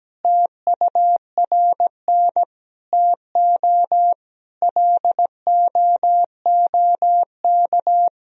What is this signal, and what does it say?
Morse code, heard as a keyed tone, about 17 wpm, 700 Hz. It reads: TURN TO LOOK
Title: Sentences from Top 200 Words and Abbreviations 17wpm